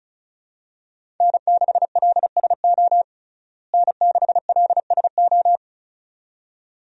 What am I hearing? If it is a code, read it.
Morse code, heard as a keyed tone, about 34 wpm, 700 Hz. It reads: N6LSO N6LSO
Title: USA Call Signs - 5 Characters - Repeated 2x - 35wpm